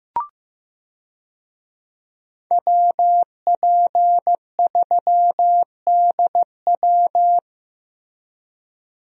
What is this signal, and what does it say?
Morse code, heard as a keyed tone, about 16 wpm, 700 Hz. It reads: WP3DW
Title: USA Call Signs - 5 Characters - Repeated 2x - 15wpm